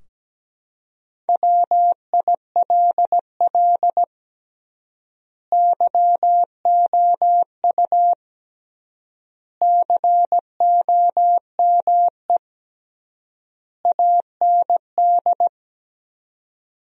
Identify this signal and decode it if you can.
Morse code, heard as a keyed tone, about 17 wpm, 700 Hz. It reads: WILL YOU COME AND
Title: 3x Word-Spacing - Sentences from Top 100 Words - 17wpm